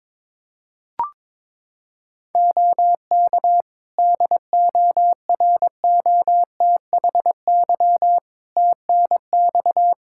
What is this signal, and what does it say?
Morse code, heard as a keyed tone, about 22 wpm, 700 Hz. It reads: OK DOROTHY TNX
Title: Practice QSO - Exchange 4 - Ending - 22wpm